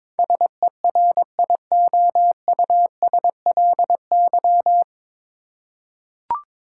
Morse code, heard as a keyed tone, about 22 wpm, 700 Hz. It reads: SERIOUSLY
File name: 9 Letter Words - First 3 Letters Given - 22wpm